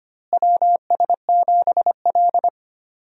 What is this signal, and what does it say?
Morse code, heard as a keyed tone, about 25 wpm, 700 Hz. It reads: WS7L
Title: USA Call Signs - 4 Characters - Repeated 2x - 25wpm